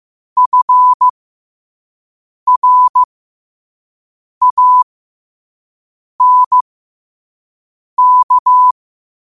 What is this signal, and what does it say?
Morse code, heard as a keyed tone, about 15 wpm, spaced as for 5 wpm, 1000 Hz. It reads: FRANK
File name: qotdfiles / 5wpm